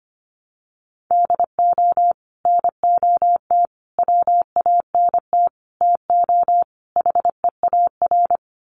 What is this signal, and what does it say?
Morse code, heard as a keyed tone, about 25 wpm, 700 Hz. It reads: DO NOT WANT TO HEAR
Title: Sets of 5 Words 25wpm